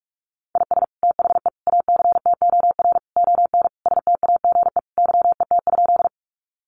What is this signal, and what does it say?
Morse code, heard as a keyed tone, about 44 wpm, 700 Hz. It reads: IS THE ACTOR ON STAGE YET?
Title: Sentences from Top 1500 Words 45wpm